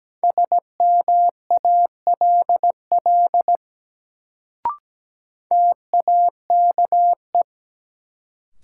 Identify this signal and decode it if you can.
Morse code, heard as a keyed tone, about 17 wpm, 700 Hz. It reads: SMALL TAKE
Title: Sets of 2 Words - New Word Drill - 17wpm